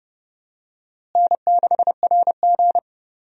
Morse code, heard as a keyed tone, about 30 wpm, 700 Hz. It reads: N6RG
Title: USA Call Signs - 4 Characters - Repeated 2x - 30wpm